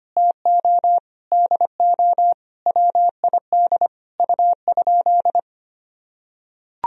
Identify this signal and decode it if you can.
Morse code, heard as a keyed tone, about 25 wpm, 700 Hz. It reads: TO DO WID U?
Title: Sentences from Top 100 Words and Abbreviations 25wpm